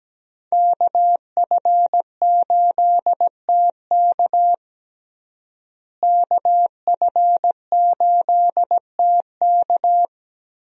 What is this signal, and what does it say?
Morse code, heard as a keyed tone, about 17 wpm, 700 Hz. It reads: KF8TK KF8TK
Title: USA Call Signs - 5 Characters - Repeated 3x - 17wpm